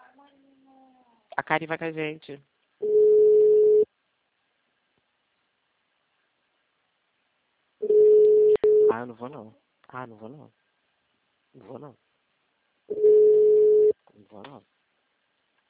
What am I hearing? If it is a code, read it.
Ringback tone (Europe, 425 Hz), with speech.